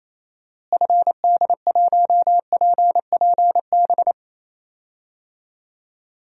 Morse code, heard as a keyed tone, about 28 wpm, 700 Hz. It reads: FD1PPB
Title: DXCC-100 Call Signs - 6 Characters - Repeated 3x - 28wpm